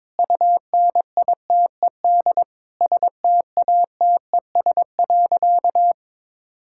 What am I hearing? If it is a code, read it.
Morse code, heard as a keyed tone, about 22 wpm, 700 Hz. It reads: UNITED STATES.
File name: practice / 22wpm / Sentences with 8-Word Phrases - 22wpm